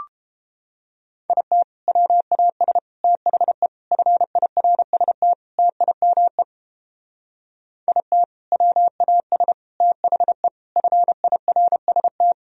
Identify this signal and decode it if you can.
Morse code, heard as a keyed tone, about 31 wpm, 700 Hz. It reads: IT WAS THE FIRST TIME IT WAS THE FIRST